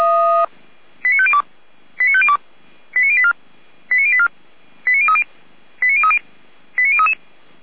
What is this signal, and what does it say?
Five-tone selective call (ZVEI), heard as quick runs of five tones, tones of about 70 ms. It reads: 89502 89502 89084 89084 89030 89030 89033